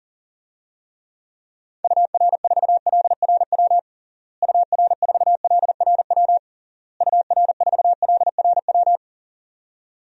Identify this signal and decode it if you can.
Morse code, heard as a keyed tone, about 40 wpm, 700 Hz. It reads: UR4LRW UR4LRW UR4LRW
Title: DXCC-100 Call Signs - 6 Characters - Repeated 3x - 40wpm